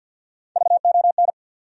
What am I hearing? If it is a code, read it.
Morse code, heard as a keyed tone, about 47 wpm, 700 Hz. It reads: VON
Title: USA Call Sign Postfixes - 3 Letters 50wpm